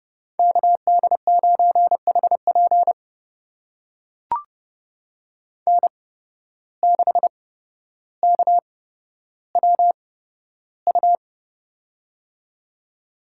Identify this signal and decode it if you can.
Morse code, heard as a keyed tone, about 32 wpm, spaced as for 13 wpm, 700 Hz. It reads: KD9HP N6KWU
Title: USA Call Signs - 5 Characters - 8x Character Spacing - 30wpm